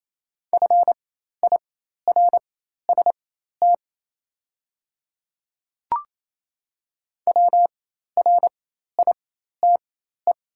Morse code, heard as a keyed tone, about 28 wpm, spaced as for 13 wpm, 700 Hz. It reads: FIRST WRITE
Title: 4x Character Spacing - Top 100 Words - 28wpm